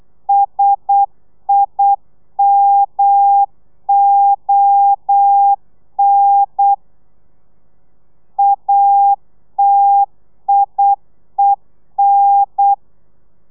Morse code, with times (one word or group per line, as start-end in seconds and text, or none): 0.3-6.7 SIMON
8.4-12.7 ATIEN